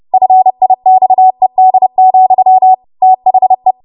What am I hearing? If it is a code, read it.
Morse code, heard as a keyed tone, about 30 wpm, 750 Hz. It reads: FIXED, THE
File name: Archive / 30 WPM / 30WPM